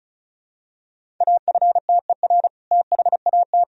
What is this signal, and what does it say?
Morse code, heard as a keyed tone, about 34 wpm, 700 Hz. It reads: AFTER THAT
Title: Sentences from Top 1000 Words - Review Part 1 - 35wpm